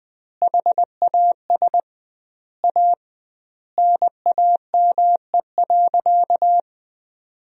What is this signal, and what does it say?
Morse code, heard as a keyed tone, about 20 wpm, 700 Hz. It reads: HAS A NAME.
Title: 2x Word-Spacing - Sentences from Top 500 Words - 20wpm